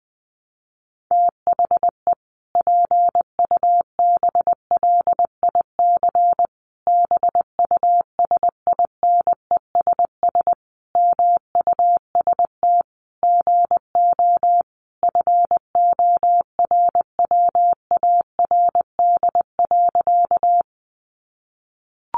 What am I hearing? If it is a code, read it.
Morse code, heard as a keyed tone, about 20 wpm, 700 Hz. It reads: THE PUBLIC BUSINESS MUST GO FORWARD.
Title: Sentences from Top 1300 Words 20wpm